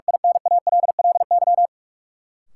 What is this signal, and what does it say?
Morse code, heard as a keyed tone, about 46 wpm, 700 Hz. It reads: INALLY
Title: Sentences from Top 1500 Words - New Word Drill - 45wpm